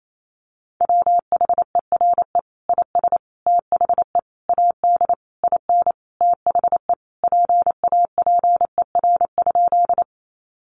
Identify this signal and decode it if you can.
Morse code, heard as a keyed tone, about 28 wpm, 700 Hz. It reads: WHERE IS THE AD IN THE PAPER?
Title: Sentences from Top 1300 Words 28wpm